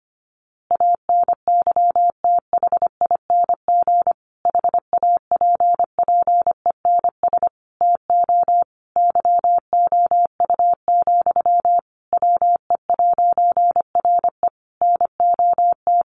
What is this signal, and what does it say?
Morse code, heard as a keyed tone, about 25 wpm, 700 Hz. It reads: ANYTHING HAPPENS TO YOU, WE'RE NOT
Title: A Sound Of Thunder 25wpm